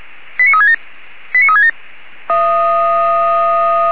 Five-tone selective call (ZVEI), heard as quick runs of five tones, tones of about 70 ms. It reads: 89367 89367